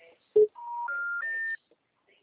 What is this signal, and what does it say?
Special information tone.